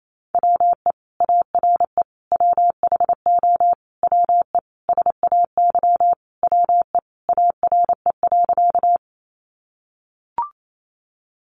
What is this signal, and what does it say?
Morse code, heard as a keyed tone, about 28 wpm, 700 Hz. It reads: WE ARE WHO WE SAY WE ARE.
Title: Sentences from Top 200 Words 28wpm